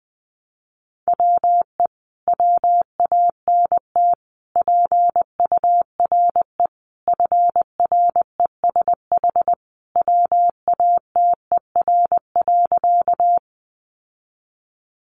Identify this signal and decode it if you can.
Morse code, heard as a keyed tone, about 20 wpm, 700 Hz. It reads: WE WANT PURE FRESH WATER.
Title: Sentences from Top 1500 Words 20wpm